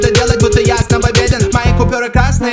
There is no code.